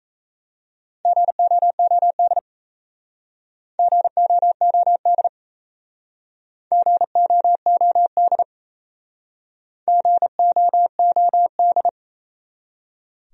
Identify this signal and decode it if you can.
Morse code, heard as a keyed tone, about 36 wpm, 700 Hz. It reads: GOOD GOOD GOOD GOOD